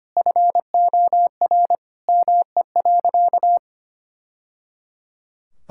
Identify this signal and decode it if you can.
Morse code, heard as a keyed tone, about 25 wpm, 700 Hz. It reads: FOR ME.
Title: Sentences from Top 1000 Words - Review Part 6 - 25wpm